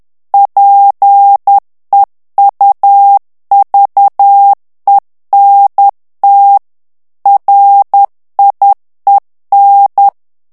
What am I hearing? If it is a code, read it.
Morse code, heard as a keyed tone, about 10 wpm, 800 Hz. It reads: PEUVENT RIEN
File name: CW / FICHIERS-MP3 / FRANCAIS TEXTE CLAIR 2 VITESSE 10 MOTS MINUTES